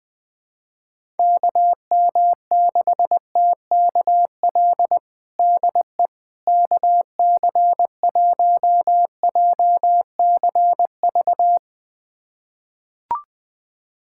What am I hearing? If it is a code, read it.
Morse code, heard as a keyed tone, about 20 wpm, 700 Hz. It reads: KM6TKL DE KC1JCV
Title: USA Callsign DE Callsign 20wpm